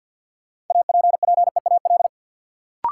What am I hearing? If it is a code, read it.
Morse code, heard as a keyed tone, about 52 wpm, 700 Hz. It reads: APPEAL
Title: Sentences from Top 1300 Words - New Word Drill - 50wpm